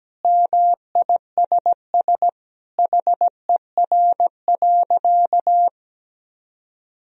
Morse code, heard as a keyed tone, about 17 wpm, 700 Hz. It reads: MISS HER.
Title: Sentences from Top 1000 Words - Review Part 3 - 17wpm